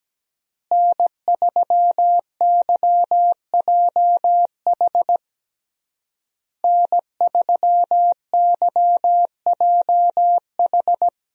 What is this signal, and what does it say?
Morse code, heard as a keyed tone, about 17 wpm, 700 Hz. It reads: N3YJH N3YJH
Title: USA Call Signs - 5 Characters - Repeated 3x - 17wpm